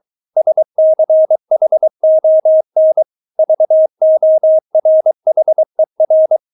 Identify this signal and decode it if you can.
Morse code, heard as a keyed tone, about 23 wpm, 630 Hz. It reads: SCHON VORHER
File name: Witz4 23wpm